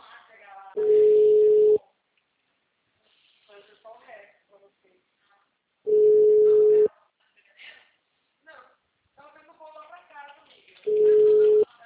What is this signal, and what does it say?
Ringback tone (Europe, 425 Hz), with speech.